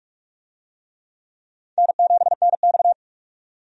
Morse code, heard as a keyed tone, about 46 wpm, 700 Hz. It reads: N7NX
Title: USA Call Signs - 4 Characters - 45wpm